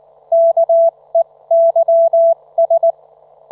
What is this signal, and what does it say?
Morse code, heard as a keyed tone, about 19 wpm, spaced as for 17 wpm, 670 Hz. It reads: KEYS
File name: Trivia